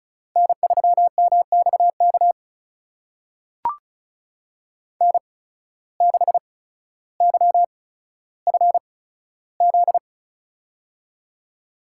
Morse code, heard as a keyed tone, about 36 wpm, spaced as for 15 wpm, 700 Hz. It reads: N3MXK N6YFZ
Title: USA Call Signs - 5 Characters - 8x Character Spacing - 35wpm